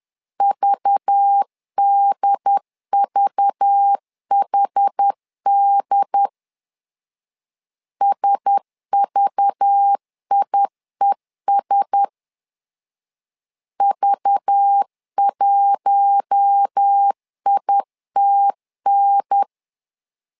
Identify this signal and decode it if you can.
Morse code, heard as a keyed tone, about 10 wpm, 800 Hz. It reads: VDVHD SVIES V1ITN
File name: LEZIONI / 5 LBVC / 5 LBV 10wpm 2248